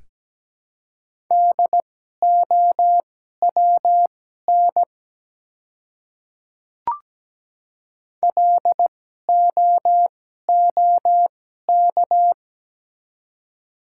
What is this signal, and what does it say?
Morse code, heard as a keyed tone, about 17 wpm, spaced as for 12 wpm, 700 Hz. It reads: DOWN LOOK
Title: 2x Character Spacing - Top 100 Words - 17wpm